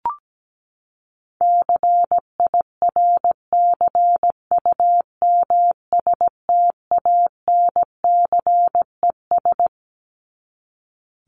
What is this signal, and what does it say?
Morse code, heard as a keyed tone, about 17 wpm, 700 Hz. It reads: CIRCUMSTANCES
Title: ES:S Suffix - Top 100 Words with Base Words 17wpm